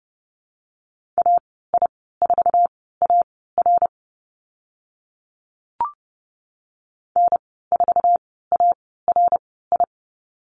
Morse code, heard as a keyed tone, about 30 wpm, spaced as for 17 wpm, 700 Hz. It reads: AI4AR N4ARI
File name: Lesson 14g - Number 2 - Call Signs 30wpm